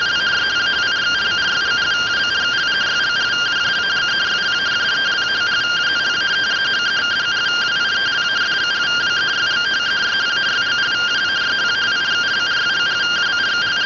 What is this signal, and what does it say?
Two-tone teleprinter signal, about 170 Hz shift.